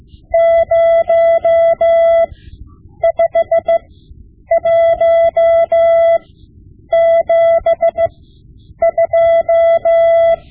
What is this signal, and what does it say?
Morse code, heard as a keyed tone, about 15 wpm, spaced as for 8 wpm, 670 Hz. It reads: 05172
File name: Morse Lessons / Morse Lessons / Tape-5B